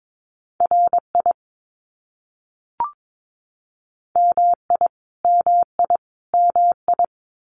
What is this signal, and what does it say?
Morse code, heard as a keyed tone, about 22 wpm, 700 Hz. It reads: RI MI MI MI
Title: US States Repeated 3x 22wpm